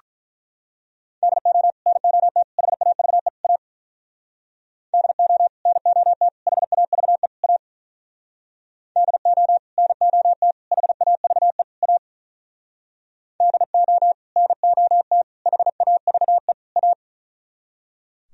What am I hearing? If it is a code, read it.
Morse code, heard as a keyed tone, about 43 wpm, 700 Hz. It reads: DO NOT HAVE A DO NOT HAVE A DO NOT HAVE A DO NOT HAVE A
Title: Speed-Racing - Sets of 4 Words 35wpm